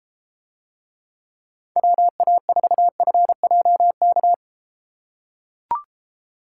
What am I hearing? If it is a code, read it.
Morse code, heard as a keyed tone, about 32 wpm, 700 Hz. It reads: WA4FJK